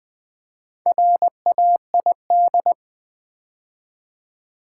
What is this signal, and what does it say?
Morse code, heard as a keyed tone, about 20 wpm, 700 Hz. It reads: RAID
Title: Build A Word - 2 Letters At A Time - 4-Letter Words - 20wpm